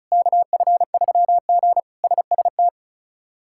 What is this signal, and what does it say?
Morse code, heard as a keyed tone, about 36 wpm, 700 Hz. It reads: KF3G SST
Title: K1USN SST QSO Practice - 35wpm